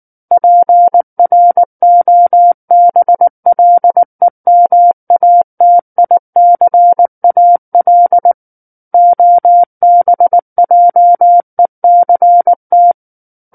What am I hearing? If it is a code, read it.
Morse code, heard as a keyed tone, about 19 wpm, 700 Hz. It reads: PROBLEMATICAL OBJECT